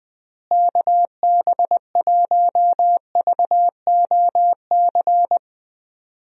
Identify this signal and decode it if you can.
Morse code, heard as a keyed tone, about 20 wpm, 700 Hz. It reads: KB1VOC
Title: USA Call Signs - 6 Characters - Repeated 2x - 20wpm